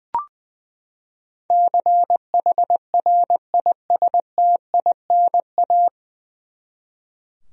Morse code, heard as a keyed tone, about 20 wpm, 700 Hz. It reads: CHRISTINA
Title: USA Common Names - 20wpm